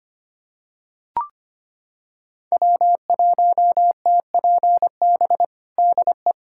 Morse code, heard as a keyed tone, about 25 wpm, 700 Hz. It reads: W1TPB DE
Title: USA Callsign DE Callsign - 5 Characters - 25wpm